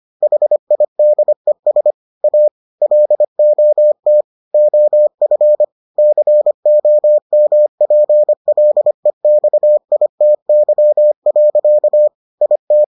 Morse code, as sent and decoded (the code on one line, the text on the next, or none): .... .. -.. ... / .- / .-.. --- - / --- ..-. / -.-. --- -- .--. .-.. -..- .. - -.-- .-.-.- / .. -
HIDES A LOT OF COMPLEXITY. IT